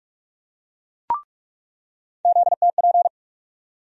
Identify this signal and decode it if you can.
Morse code, heard as a keyed tone, about 44 wpm, 700 Hz. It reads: ZTP
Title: USA Call Sign Postfixes - 3 Letters 45wpm